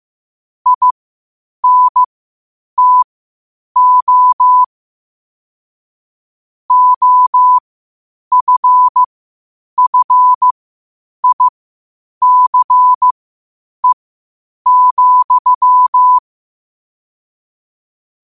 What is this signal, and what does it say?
Morse code, heard as a keyed tone, about 15 wpm, spaced as for 8 wpm, 1000 Hz. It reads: INTO OFFICE,
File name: qotdfiles / 8wpm